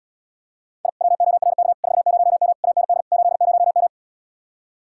Morse code, heard as a keyed tone, about 79 wpm, 700 Hz. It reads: I WORK 40M AND 80M